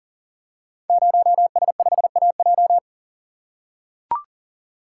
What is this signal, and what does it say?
Morse code, heard as a keyed tone, about 39 wpm, 700 Hz. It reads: 0S5AJ
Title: Code Groups - 5 Letter:Numbers - 40wpm